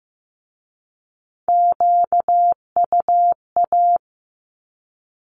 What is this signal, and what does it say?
Morse code, heard as a keyed tone, about 15 wpm, 700 Hz. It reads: QUA
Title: Q-Codes with Spoken Meaning - 15wpm